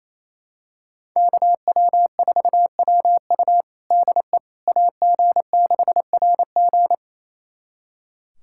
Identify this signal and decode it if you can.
Morse code, heard as a keyed tone, about 28 wpm, 700 Hz. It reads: KW4WU DE AG6RG
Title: USA Callsign DE Callsign - 5 Characters - 28wpm